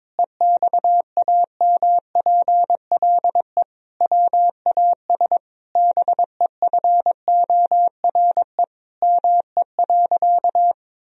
Morse code, heard as a keyed tone, about 22 wpm, 700 Hz. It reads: EXAMPLE WAS BEFORE ME.
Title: Sentences from Top 1000 Words - Review Part 1 - 22wpm